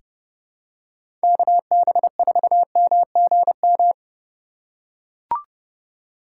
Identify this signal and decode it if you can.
Morse code, heard as a keyed tone, about 30 wpm, 700 Hz. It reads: KB4MGM
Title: USA Call Signs - 6 Characters - 3x Character Spacing - 30wpm